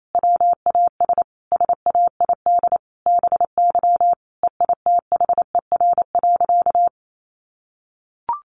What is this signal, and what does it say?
Morse code, heard as a keyed tone, about 28 wpm, 700 Hz. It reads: WAS SAID BY EITHER.
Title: Sentences from Top 600 Words 28wpm